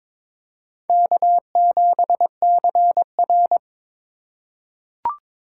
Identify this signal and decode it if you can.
Morse code, heard as a keyed tone, about 22 wpm, 700 Hz. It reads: K7CR
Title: USA Call Signs - 4 Characters - 3x Character Spacing - 22wpm